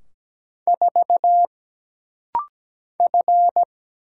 Morse code, heard as a keyed tone, about 17 wpm, 700 Hz. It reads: 4 F
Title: Single Letter-Number - Rapid-Fire 17wpm